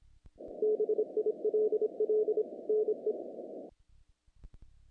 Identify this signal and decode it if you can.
Morse code, heard as a keyed tone, about 25 wpm, 430 Hz. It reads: BILL NE
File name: contests / naqp-cw-k5tr-op-k5pi- -1800